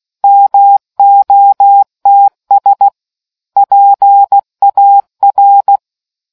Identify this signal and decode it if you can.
Morse code, heard as a keyed tone, about 16 wpm, 800 Hz. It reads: MOTS PAR